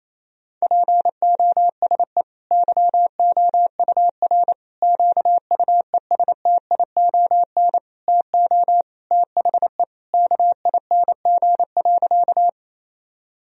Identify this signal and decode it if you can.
Morse code, heard as a keyed tone, about 27 wpm, 700 Hz. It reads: POSE YOUR QUESTION TO THE KING.
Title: Sentences from Top 1000 Words - Review Part 2 - 28wpm